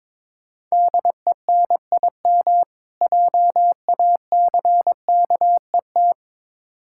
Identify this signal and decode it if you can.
Morse code, heard as a keyed tone, about 22 wpm, 700 Hz. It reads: DENIM JACKET
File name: Difficult Words with Hints - 22wpm